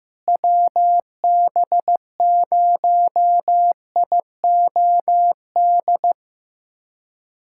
Morse code, heard as a keyed tone, about 15 wpm, 700 Hz. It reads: WB0IOD
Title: USA Callsign DE Callsign 15wpm